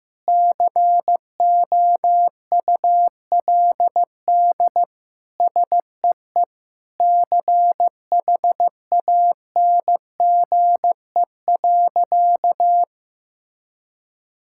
Morse code, heard as a keyed tone, about 15 wpm, 700 Hz. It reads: COULD SEE CHANGE.
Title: Sentences from Top 1000 Words - Review Part 5 - 15wpm